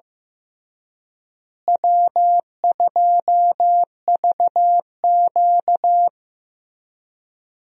Morse code, heard as a keyed tone, about 15 wpm, 700 Hz. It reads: W2VQ
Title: USA Call Signs - 4 Characters - Repeated 3x - 15wpm